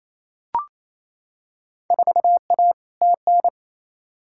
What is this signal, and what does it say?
Morse code, heard as a keyed tone, about 28 wpm, 700 Hz. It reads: 4A TN
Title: Field Day Exchanges Encore - 28wpm